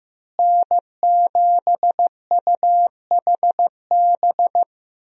Morse code, heard as a keyed tone, about 15 wpm, 700 Hz. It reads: N7UHB
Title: USA Call Signs - 5 Characters - Repeated 3x - 15wpm